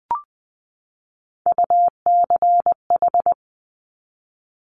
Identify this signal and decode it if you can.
Morse code, heard as a keyed tone, about 20 wpm, 700 Hz. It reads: UCH
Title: Top Three Letter Combinations in Words 20wpm